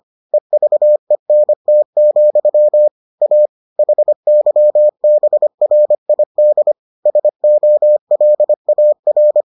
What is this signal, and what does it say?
Morse code, heard as a keyed tone, about 25 wpm, 600 Hz. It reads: EVENT, A HYBRID SOLAR